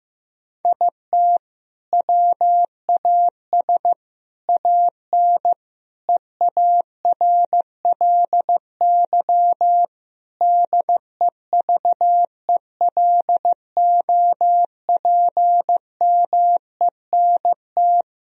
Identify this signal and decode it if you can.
Morse code, heard as a keyed tone, about 15 wpm, 700 Hz. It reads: IT WAS AN EARLY DEVELOPMENT
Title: Sentences from Top 1000 Words - Review Part 4 - 15wpm